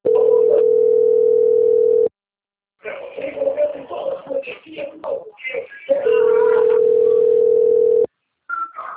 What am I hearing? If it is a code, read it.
Ringback tone (North America), 440 + 480 Hz, with speech.
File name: out- - - - - .50685